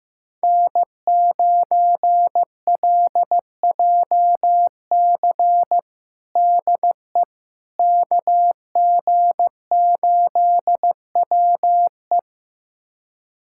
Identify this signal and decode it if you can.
Morse code, heard as a keyed tone, about 15 wpm, 700 Hz. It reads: N9LJC DE KG8WE
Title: USA Callsign DE Callsign - 5 Characters - 15wpm